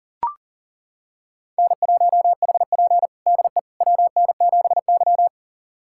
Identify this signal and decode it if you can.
Morse code, heard as a keyed tone, about 40 wpm, 690 Hz. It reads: N1HP DE WN7Y
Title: USA Callsign DE Callsign - 4 Characters - 40wpm